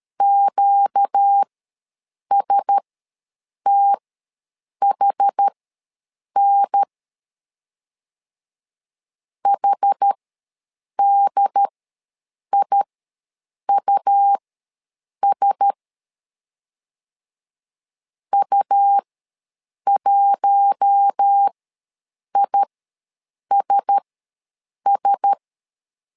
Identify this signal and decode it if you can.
Morse code, heard as a keyed tone, about 13 wpm, spaced as for 7 wpm, 800 Hz. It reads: QSTHN HDIUS U1ISS
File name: LEZIONI / 4 Q1S ws 12wpm 1827